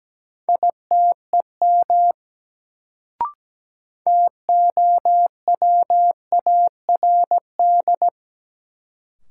Morse code, heard as a keyed tone, about 17 wpm, 700 Hz. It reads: ITEM TOWARD
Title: Sentences from Top 1000 Words - Review Part 6 - New Word Drill - 17wpm